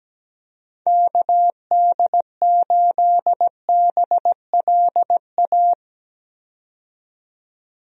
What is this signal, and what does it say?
Morse code, heard as a keyed tone, about 17 wpm, 700 Hz. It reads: KD8BLA
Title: USA Call Signs - 6 Characters - 17wpm